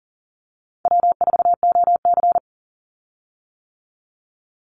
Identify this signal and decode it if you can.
Morse code, heard as a keyed tone, about 40 wpm, 700 Hz. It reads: W4OC
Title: Easy Call Signs - 40wpm